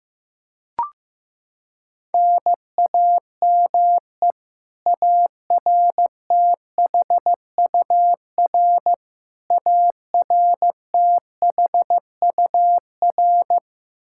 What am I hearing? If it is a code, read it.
Morse code, heard as a keyed tone, about 15 wpm, 700 Hz. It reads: NAME ARTHUR ARTHUR
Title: Practice QSO - Exchange 1 - Beginning - 15wpm